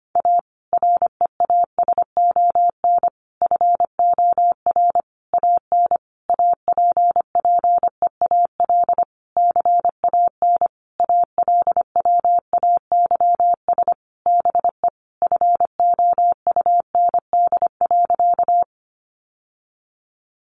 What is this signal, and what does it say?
Morse code, heard as a keyed tone, about 25 wpm, 700 Hz. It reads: A REASON FOR AN APPEAL CAN ALWAYS BE FOUND.
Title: Sentences from Top 1300 Words 25wpm